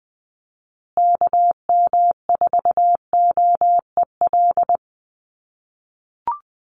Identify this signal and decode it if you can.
Morse code, heard as a keyed tone, about 20 wpm, 700 Hz. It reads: KM4OEL
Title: USA Call Signs Repeated 2x 20wpm